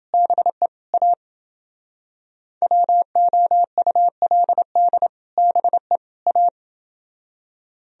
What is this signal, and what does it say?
Morse code, heard as a keyed tone, about 30 wpm, 700 Hz. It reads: BE A WOULD BE A